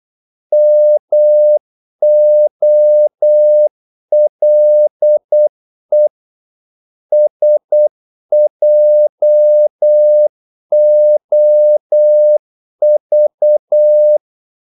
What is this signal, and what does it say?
Morse code, heard as a keyed tone, about 8 wpm, 600 Hz. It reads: MOLE SJOV
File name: Korte ord DK 8wpm